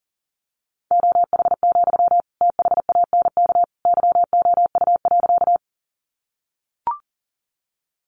Morse code, heard as a keyed tone, about 40 wpm, 700 Hz. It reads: OH, THANK YOU.
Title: Sentences from Top 600 Words 40wpm